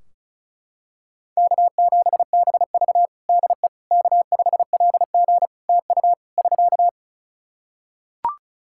Morse code, heard as a keyed tone, about 36 wpm, 700 Hz. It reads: K7BV DE K5LG TU <SK>
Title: K1USN SST QSO Practice 35wpm